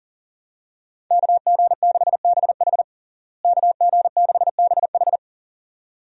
Morse code, heard as a keyed tone, about 40 wpm, 700 Hz. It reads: KG6BH KG6BH
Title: USA Call Signs - 5 Characters - Repeated 2x - 40wpm